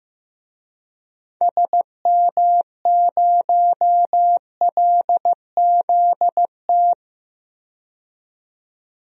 Morse code, heard as a keyed tone, about 15 wpm, 700 Hz. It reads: SM0LZT
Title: DXCC-100 Call Signs - 6 Characters - 15wpm